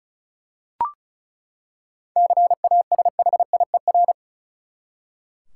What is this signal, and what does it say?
Morse code, heard as a keyed tone, about 36 wpm, 700 Hz. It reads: CASHIER
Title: Practice QSO - Exchange 3 - More Info - 35wpm